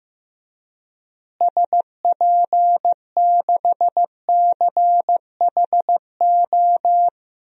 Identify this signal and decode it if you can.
Morse code, heard as a keyed tone, about 15 wpm, 700 Hz. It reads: SP6CHO
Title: DXCC-100 Call Signs - 6 Characters - Repeated 3x - 15wpm